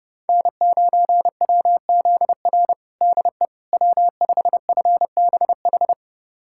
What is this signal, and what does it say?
Morse code, heard as a keyed tone, about 32 wpm, 700 Hz. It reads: N9WZR DE W5FBH
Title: USA Callsign DE Callsign - 5 Characters - 30wpm